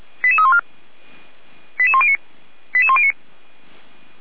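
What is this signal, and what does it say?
Five-tone selective call (ZVEI), heard as quick runs of five tones, tones of about 70 ms. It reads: 88314 88109 88109
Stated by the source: ZVEI five-tone call received on an SDR.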